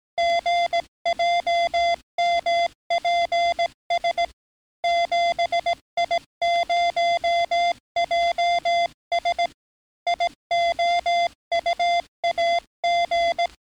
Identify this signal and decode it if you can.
Morse code, heard as a keyed tone, about 17 wpm, 690 Hz. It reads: GJMPS 7I0JS IOUAG
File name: training / NAVPERS 11049to11060 / NAVPERS 11057 RA RB